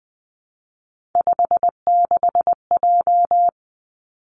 Morse code, heard as a keyed tone, about 19 wpm, 700 Hz. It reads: 56J
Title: Three Challenging Letter-Number Combinations 20wpm